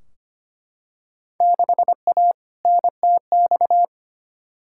Morse code, heard as a keyed tone, about 25 wpm, 700 Hz. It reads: 6A NTX
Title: Field Day Exchanges - 25wpm